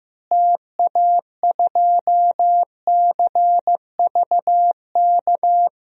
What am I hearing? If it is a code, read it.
Morse code, heard as a keyed tone, about 15 wpm, 700 Hz. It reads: TA2CVK
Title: DXCC-100 Call Signs - 6 Characters - Repeated 3x - 15wpm